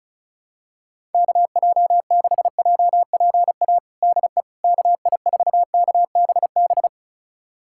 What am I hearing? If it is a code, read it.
Morse code, heard as a keyed tone, about 36 wpm, 700 Hz. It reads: KJ6JPA DE KI4KBB
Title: USA Callsign DE Callsign - 6 Characters - 35wpm